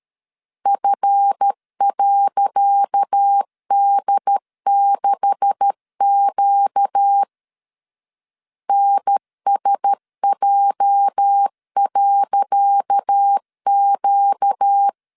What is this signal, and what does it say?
Morse code, heard as a keyed tone, about 12 wpm, 800 Hz. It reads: F.D6Q NSJ.Q
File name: LEZIONI / 10 SIGNS / 10C DOT 12wpm 2451